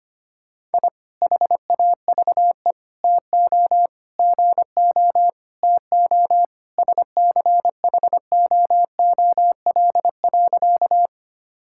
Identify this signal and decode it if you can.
Morse code, heard as a keyed tone, about 25 wpm, 700 Hz. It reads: I HAVE TO GO TO SCHOOL.
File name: practice / 25wpm / Sentences with 3-Word Phrases - 25wpm